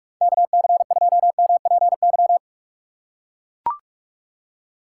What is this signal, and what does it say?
Morse code, heard as a keyed tone, about 41 wpm, 700 Hz. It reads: KC2MPY
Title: USA Call Signs - 6 Characters - 6x Character Spacing - 45wpm